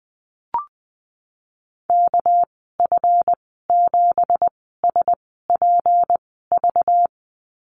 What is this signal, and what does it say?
Morse code, heard as a keyed tone, about 20 wpm, spaced as for 15 wpm, 700 Hz. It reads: KF7SPV
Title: 207 Lesson 33g - Letter j - Call Signs 20wpm